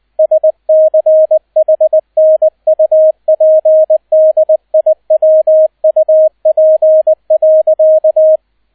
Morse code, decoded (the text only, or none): SCHNUPDIWUP.